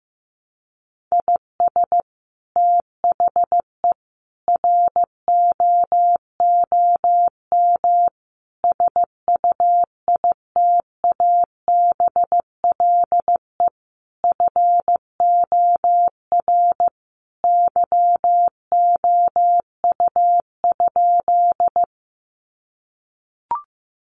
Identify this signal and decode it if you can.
Morse code, heard as a keyed tone, about 15 wpm, 700 Hz. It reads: IS THE ROOM SUITABLE FOR YOU?
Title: Sentences from Top 1400 Words 15wpm